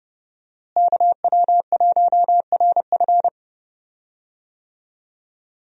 Morse code, heard as a keyed tone, about 31 wpm, 700 Hz. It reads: KW1RF
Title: USA Call Signs - 5 Characters - Repeated 2x - 30wpm